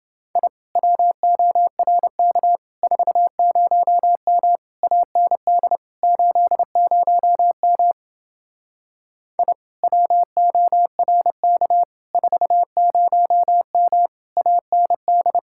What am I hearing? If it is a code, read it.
Morse code, heard as a keyed tone, about 29 wpm, 700 Hz. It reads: I WORK 40M AND 80M I WORK 40M AND